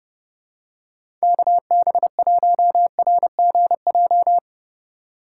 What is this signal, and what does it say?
Morse code, heard as a keyed tone, about 30 wpm, 700 Hz. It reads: KB1RGJ
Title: USA Call Signs - 6 Characters - 4x Character Spacing - 30wpm